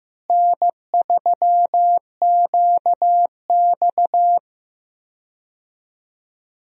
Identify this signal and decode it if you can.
Morse code, heard as a keyed tone, about 15 wpm, 700 Hz. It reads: N3QX
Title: USA Call Signs - 4 Characters - Repeated 3x - 15wpm